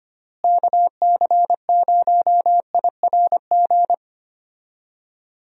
Morse code, heard as a keyed tone, about 25 wpm, 700 Hz. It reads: KC0IRG
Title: USA Call Signs - 6 Characters - Repeated 2x - 25wpm